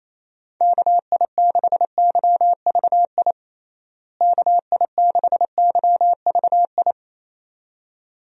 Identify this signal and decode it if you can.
Morse code, heard as a keyed tone, about 28 wpm, 700 Hz. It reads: KI6YVI KI6YVI
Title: USA Call Signs - 6 Characters - Repeated 2x - 28wpm